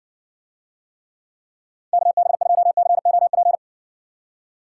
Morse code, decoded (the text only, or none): KB2COP